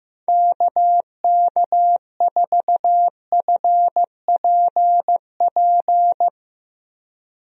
Morse code, heard as a keyed tone, about 15 wpm, 700 Hz. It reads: KK4FPP
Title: USA Call Signs - 6 Characters - Repeated 3x - 15wpm